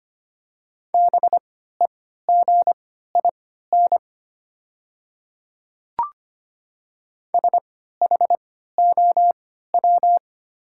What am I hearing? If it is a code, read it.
Morse code, heard as a keyed tone, about 25 wpm, spaced as for 14 wpm, 700 Hz. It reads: BEGIN SHOW
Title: 3x Character Spacing - Top 300 Words - 25wpm